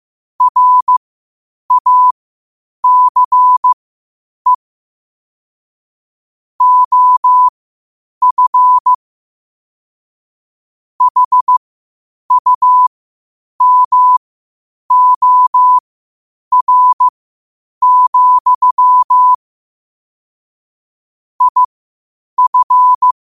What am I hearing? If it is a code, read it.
Morse code, heard as a keyed tone, about 15 wpm, spaced as for 8 wpm, 1000 Hz. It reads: RACE OF HUMOR, IF
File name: qotdfiles / 8wpm